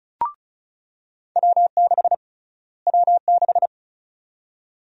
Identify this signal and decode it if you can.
Morse code, heard as a keyed tone, about 36 wpm, 700 Hz. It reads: W6 W6
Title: USA Call Sign Prefixes - Repeated 2x - 35wpm